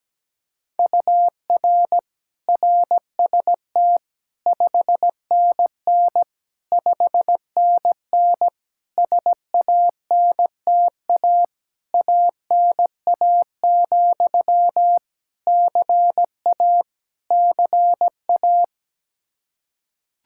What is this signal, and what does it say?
Morse code, heard as a keyed tone, about 17 wpm, 700 Hz. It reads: UR RST 5NN 5NN SANTA ANA, CA CA
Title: Practice QSO - Exchange 1 - Beginning - 17wpm